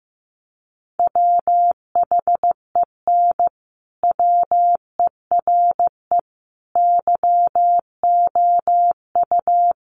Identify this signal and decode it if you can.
Morse code, heard as a keyed tone, about 15 wpm, 700 Hz. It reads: WHEN WERE YOU
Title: Sentences from Top 1300 Words 15wpm